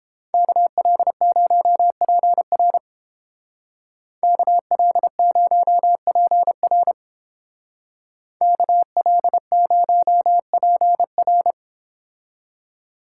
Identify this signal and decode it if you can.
Morse code, heard as a keyed tone, about 28 wpm, 700 Hz. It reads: KL0PR KL0PR KL0PR